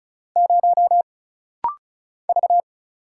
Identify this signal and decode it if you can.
Morse code, heard as a keyed tone, about 34 wpm, 700 Hz. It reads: 0 V
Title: Single Letter-Number Alt - Rapid-Fire 35wpm